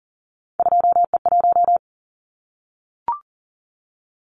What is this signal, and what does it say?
Morse code, heard as a keyed tone, about 40 wpm, 700 Hz. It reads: 2E1
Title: Callsign Prefixes for Top 25 DXCC - Repeated 3x - 40wpm